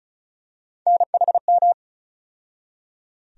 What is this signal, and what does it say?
Morse code, heard as a keyed tone, about 36 wpm, 700 Hz. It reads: NHM
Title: USA Call Sign Postfixes - 3 Letters 35wpm